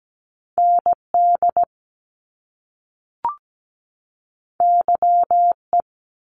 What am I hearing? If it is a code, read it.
Morse code, heard as a keyed tone, about 17 wpm, 700 Hz. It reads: ND YE
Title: Top Two Letter Combinations in Words 17wpm